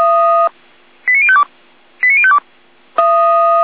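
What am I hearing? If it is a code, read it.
Five-tone selective call (ZVEI), heard as quick runs of five tones, tones of about 70 ms. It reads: 89052 89052
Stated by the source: SDR receiver recording of ZVEI five-tone selective call 89120.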